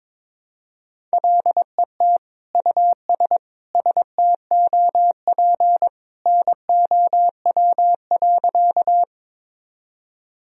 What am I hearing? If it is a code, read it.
Morse code, heard as a keyed tone, about 22 wpm, 700 Hz. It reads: LET US STOP NOW.
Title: Sentences from Top 1000 Words - Review Part 1 - 22wpm